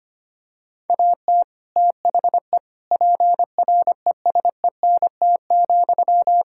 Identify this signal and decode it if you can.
Morse code, heard as a keyed tone, about 25 wpm, 700 Hz. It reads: AT THE PRESENT,
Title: Sentences from Top 1000 Words - Review Part 5 - 25wpm